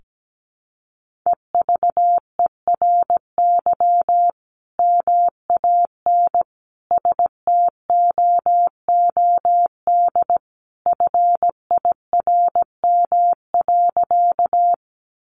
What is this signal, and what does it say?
Morse code, heard as a keyed tone, about 17 wpm, 700 Hz. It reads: EVERY MAN STOOD FIRM.
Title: Sentences from Top 1400 Words 17wpm